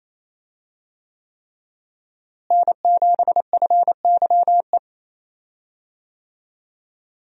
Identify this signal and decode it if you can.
Morse code, heard as a keyed tone, about 27 wpm, 700 Hz. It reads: N7FYE
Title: USA Call Signs - 5 Characters - Repeated 3x - 28wpm